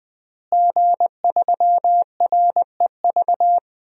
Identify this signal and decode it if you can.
Morse code, heard as a keyed tone, about 20 wpm, 700 Hz. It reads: G3REV
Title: DXCC-100 Call Signs - 5 Characters - Repeated 3x - 20wpm